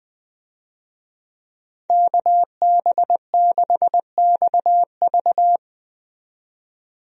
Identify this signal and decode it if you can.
Morse code, heard as a keyed tone, about 20 wpm, 700 Hz. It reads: KB6XV
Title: USA Call Signs - 5 Characters - 20wpm